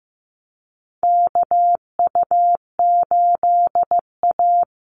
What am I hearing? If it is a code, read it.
Morse code, heard as a keyed tone, about 15 wpm, 700 Hz. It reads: KU8A
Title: USA Call Signs 15wpm